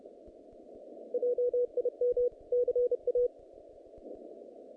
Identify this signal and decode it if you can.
Morse code, heard as a keyed tone, about 28 wpm, 490 Hz. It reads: JIM CA
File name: contests / naqp-cw-k5tr-op-k5pi- -2200